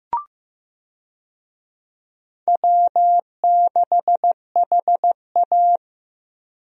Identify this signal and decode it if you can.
Morse code, heard as a keyed tone, about 15 wpm, 700 Hz. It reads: W6HA
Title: USA Call Signs - 4 Characters - Repeated 3x - 15wpm